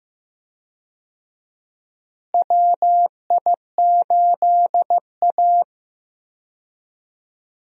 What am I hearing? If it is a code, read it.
Morse code, heard as a keyed tone, about 15 wpm, 700 Hz. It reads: WI8A
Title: USA Call Signs - 4 Characters - Repeated 3x - 15wpm